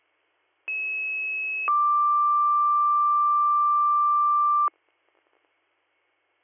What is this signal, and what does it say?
Two-tone page, heard as a one-second tone, then a three-second tone, about 2600 then 1180 Hz. It is heard from 0.7 to 4.7 s.